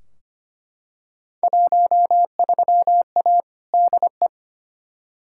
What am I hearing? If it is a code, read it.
Morse code, heard as a keyed tone, about 25 wpm, 700 Hz. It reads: 13A DE
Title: Field Day Exchanges Encore - 25wpm